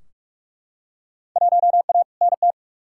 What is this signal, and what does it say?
Morse code, heard as a keyed tone, about 43 wpm, 700 Hz. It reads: 1A NT
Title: Field Day Exchanges Encore - 45wpm